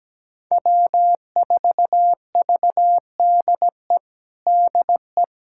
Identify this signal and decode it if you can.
Morse code, heard as a keyed tone, about 17 wpm, 700 Hz. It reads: W4VDE DE
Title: USA Callsign DE Callsign - 5 Characters - 17wpm